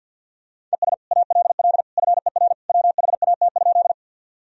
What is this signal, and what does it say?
Morse code, heard as a keyed tone, about 49 wpm, 700 Hz. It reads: ES ALL FER WHAT?
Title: Sentences from Top 100 Words and Abbreviations 50wpm